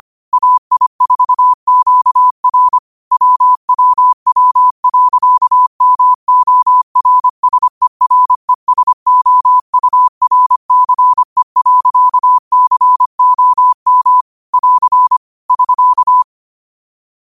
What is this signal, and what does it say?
Morse code, heard as a keyed tone, about 25 wpm, 1000 Hz. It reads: AI4QR WWW.MORSERESOURCE.COM <AR> <SK>